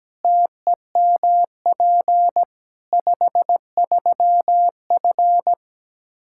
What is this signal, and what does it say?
Morse code, heard as a keyed tone, about 17 wpm, 700 Hz. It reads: TEMP 53F
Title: Practice QSO - Exchange 2 - Info - 17wpm